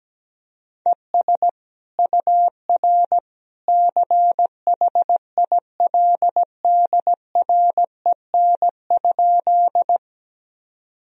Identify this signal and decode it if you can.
Morse code, heard as a keyed tone, about 17 wpm, 700 Hz. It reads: ES UR CHILDREN?
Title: Sentences from Top 300 Words and Abbreviations 17wpm